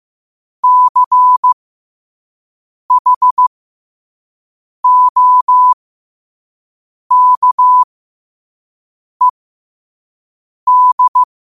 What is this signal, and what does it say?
Morse code, heard as a keyed tone, about 15 wpm, spaced as for 5 wpm, 1000 Hz. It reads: CHOKED
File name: qotdfiles / 5wpm